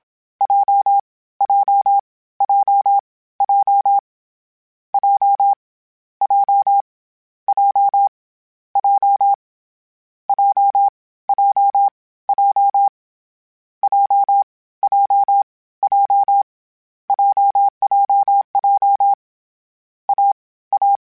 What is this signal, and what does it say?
Morse code, heard as a keyed tone, about 27 wpm, 800 Hz. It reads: J J J J J J J J J J J J J J JJJ A A
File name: CW Code Groups-mp3s / Sets mp3s / AUV sp @25